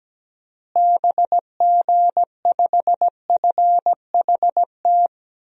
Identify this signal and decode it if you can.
Morse code, heard as a keyed tone, about 17 wpm, 700 Hz. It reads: BG5FHT
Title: DXCC-100 Call Signs - 6 Characters - Repeated 3x - 17wpm